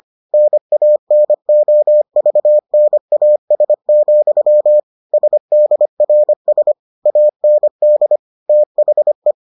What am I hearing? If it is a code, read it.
Morse code, heard as a keyed tone, about 25 wpm, 600 Hz. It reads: NANOVNAS, SDRS AND THE